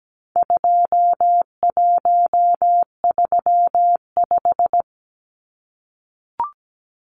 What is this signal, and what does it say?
Morse code, heard as a keyed tone, about 17 wpm, 700 Hz. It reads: 2135
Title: Numbers - 4-Digit 17wpm